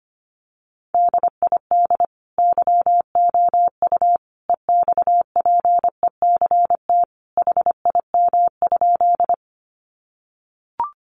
Morse code, heard as a keyed tone, about 25 wpm, 700 Hz. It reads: DID YOU EXPECT HIM?
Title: Sentences from Top 1100 Words 25wpm